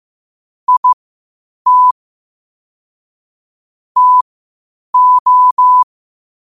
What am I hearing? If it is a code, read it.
Morse code, heard as a keyed tone, about 15 wpm, spaced as for 8 wpm, 1000 Hz. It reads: IT TO
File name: qotdfiles / 8wpm